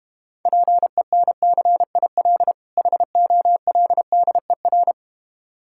Morse code, heard as a keyed tone, about 32 wpm, 700 Hz. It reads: PENCIL HOLDER